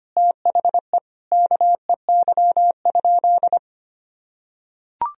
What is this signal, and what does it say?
Morse code, heard as a keyed tone, about 25 wpm, 700 Hz. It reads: THE KEY?
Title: Sentences from Top 1000 Words - Review Part 4 - 25wpm